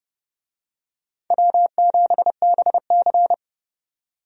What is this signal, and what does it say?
Morse code, heard as a keyed tone, about 29 wpm, 700 Hz. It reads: W7BC
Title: USA Call Signs - 4 Characters - 30wpm